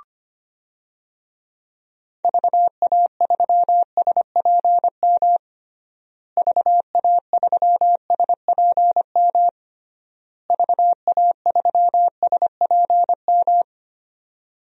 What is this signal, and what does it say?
Morse code, heard as a keyed tone, about 25 wpm, 700 Hz. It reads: VA3SPM VA3SPM VA3SPM
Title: DXCC-100 Call Signs - 6 Characters - Repeated 3x - 25wpm